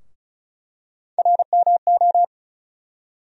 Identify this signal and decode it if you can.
Morse code, heard as a keyed tone, about 34 wpm, 700 Hz. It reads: RMO
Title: USA Call Sign Postfixes - 3 Letters 35wpm